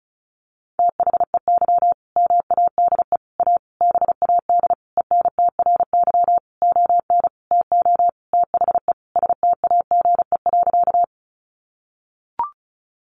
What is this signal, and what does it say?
Morse code, heard as a keyed tone, about 34 wpm, 700 Hz. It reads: THEY MADE A BAD ENTRY ON TO THE STAGE.
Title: Sentences from Top 1500 Words 35wpm